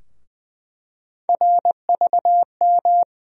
Morse code, heard as a keyed tone, about 20 wpm, 700 Hz. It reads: RVM
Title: USA Call Sign Postfixes - 3 Letters 20wpm